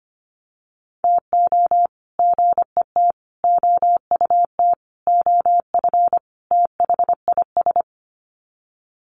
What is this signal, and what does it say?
Morse code, heard as a keyed tone, about 25 wpm, 700 Hz. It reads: TO GET OUT OF THIS
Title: Sets of 5 Words Encore2 25wpm